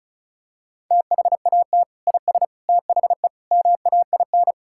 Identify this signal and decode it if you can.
Morse code, heard as a keyed tone, about 36 wpm, 700 Hz. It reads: THAT IS THE MAIN
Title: Sentences from Top 1000 Words - Review Part 5 - 35wpm